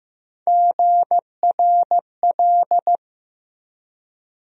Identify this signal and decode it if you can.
Morse code, heard as a keyed tone, about 15 wpm, 700 Hz. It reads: GRL
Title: USA Call Sign Postfixes - 3 Letters 15wpm